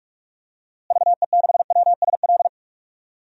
Morse code, heard as a keyed tone, about 46 wpm, 700 Hz. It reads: VE6WSL
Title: DXCC-100 Call Signs - 6 Characters - Repeated 2x - 45wpm